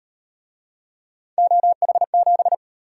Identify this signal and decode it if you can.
Morse code, heard as a keyed tone, about 36 wpm, 700 Hz. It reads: OH7